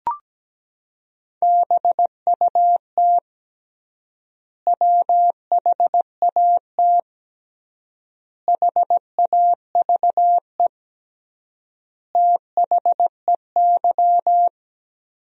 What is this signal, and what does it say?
Morse code, heard as a keyed tone, about 17 wpm, 700 Hz. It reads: BUT WHAT HAVE THEY
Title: 3x Word-Spacing - Sentences from Top 100 Words - 17wpm